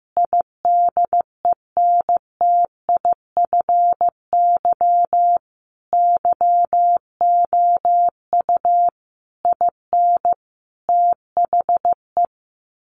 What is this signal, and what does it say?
Morse code, heard as a keyed tone, about 15 wpm, 700 Hz. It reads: IDENTIFY YOU IN THE
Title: Sentences from Top 900 Words 15wpm